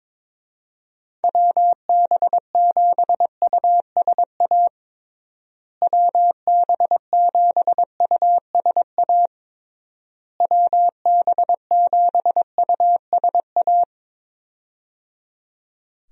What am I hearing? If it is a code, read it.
Morse code, heard as a keyed tone, about 22 wpm, 700 Hz. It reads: WB7USA WB7USA WB7USA
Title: USA Call Signs - 6 Characters - Repeated 3x - 22wpm